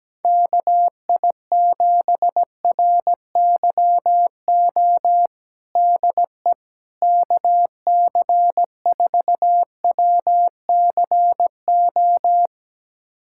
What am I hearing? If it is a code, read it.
Morse code, heard as a keyed tone, about 17 wpm, 700 Hz. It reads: KI7RYO DE KC4WCO
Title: USA Callsign DE Callsign - 6 Characters - 17wpm